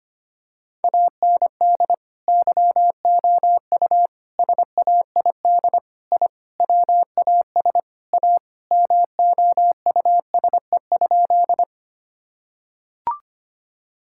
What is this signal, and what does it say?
Morse code, heard as a keyed tone, about 25 wpm, 700 Hz. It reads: AND YOU SAID I WAS A MOUSE?
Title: Sentences from Top 1000 Words - Review Part 5 - 25wpm